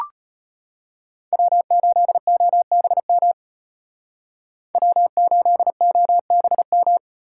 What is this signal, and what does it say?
Morse code, heard as a keyed tone, about 34 wpm, 700 Hz. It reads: W8OBM W8OBM